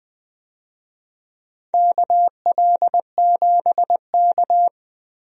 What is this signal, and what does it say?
Morse code, heard as a keyed tone, about 20 wpm, 700 Hz. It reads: KL7K
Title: DXCC-100 Call Signs - 4 Characters - Repeated 2x - 20wpm